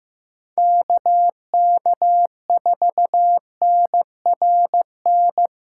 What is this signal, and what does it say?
Morse code, heard as a keyed tone, about 15 wpm, 700 Hz. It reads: KK4NRN
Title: USA Callsign DE Callsign - 6 Characters - 15wpm